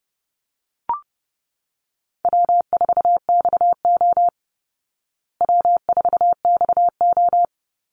Morse code, heard as a keyed tone, about 29 wpm, 700 Hz. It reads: W4XO W4XO
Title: Easy Call Signs Repeated 2x - 30wpm